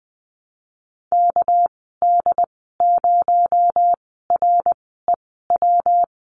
Morse code, heard as a keyed tone, about 20 wpm, spaced as for 14 wpm, 700 Hz. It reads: KD0REW
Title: 231 Lesson 37g - Letter x - Call Signs 20wpm